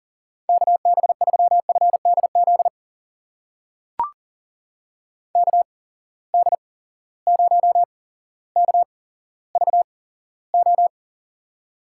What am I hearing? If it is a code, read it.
Morse code, heard as a keyed tone, about 39 wpm, spaced as for 17 wpm, 700 Hz. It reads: KB3FDZ KD0KVO
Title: USA Call Signs - 6 Characters - 8x Character Spacing - 40wpm